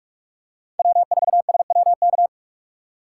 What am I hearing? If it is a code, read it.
Morse code, heard as a keyed tone, about 46 wpm, 690 Hz. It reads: W4SWK
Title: USA Call Signs - 5 Characters - 6x Character Spacing - 45wpm